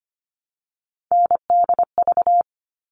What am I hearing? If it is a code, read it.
Morse code, heard as a keyed tone, about 25 wpm, 700 Hz. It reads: NDV
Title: All Cut Numbers - 3-Digit 25wpm